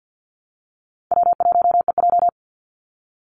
Morse code, heard as a keyed tone, about 50 wpm, 700 Hz. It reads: W1EJ
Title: NA SOTA Activator Callsigns 50wpm